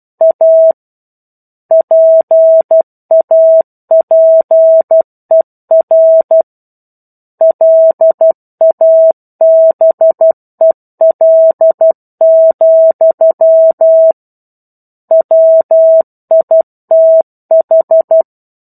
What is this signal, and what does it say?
Morse code, heard as a keyed tone, about 12 wpm, 650 Hz. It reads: A PAPER LABEL, WITH
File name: 01-Chapter-01 12 WPM